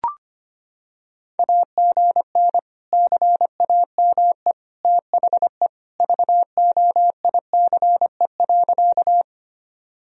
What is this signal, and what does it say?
Morse code, heard as a keyed tone, about 25 wpm, 700 Hz. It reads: AGN CAME THE VOICE.
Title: Sentences from Top 500 Words and Abbreviations 25wpm